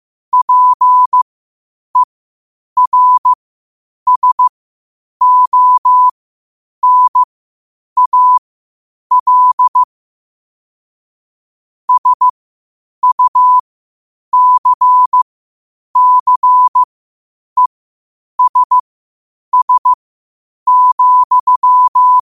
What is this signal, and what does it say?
Morse code, heard as a keyed tone, about 15 wpm, spaced as for 8 wpm, 1000 Hz. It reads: PERSONAL SUCCESS,
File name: 8wpm